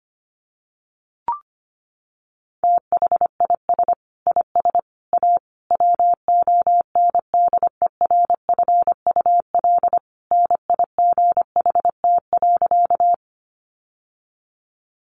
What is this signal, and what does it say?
Morse code, heard as a keyed tone, about 25 wpm, 700 Hz. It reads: THIS IS A WONDERFUL NIGHT.
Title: Sentences from Top 1100 Words 25wpm